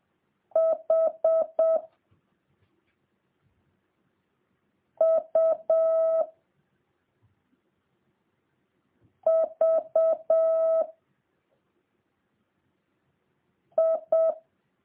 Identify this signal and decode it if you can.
Morse code, heard as a keyed tone, about 7 wpm, 650 Hz. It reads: H U V I